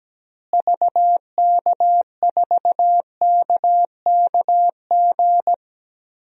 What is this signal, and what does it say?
Morse code, heard as a keyed tone, about 17 wpm, 700 Hz. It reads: VK4KKG
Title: DXCC-100 Call Signs - 6 Characters - Repeated 2x - 17wpm